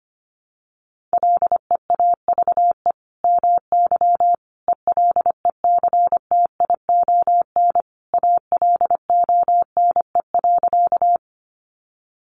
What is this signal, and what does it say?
Morse code, heard as a keyed tone, about 25 wpm, 700 Hz. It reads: LEAVE MY ELECTION ALONE.
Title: Sentences from Top 1500 Words 25wpm